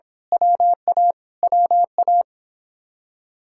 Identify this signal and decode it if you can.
Morse code, heard as a keyed tone, about 25 wpm, 700 Hz. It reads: WA WA
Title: Speed-Racing Common QSO Phrases - 22wpm